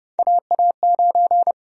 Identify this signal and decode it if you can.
Morse code, heard as a keyed tone, about 30 wpm, 700 Hz. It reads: AA9
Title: USA Call Sign Prefixes - 30wpm